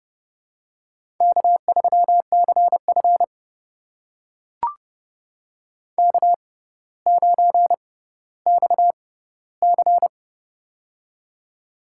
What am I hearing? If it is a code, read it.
Morse code, heard as a keyed tone, about 30 wpm, spaced as for 15 wpm, 700 Hz. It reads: K3CF K9XC
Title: USA Call Signs - 4 Characters - 6x Character Spacing - 30wpm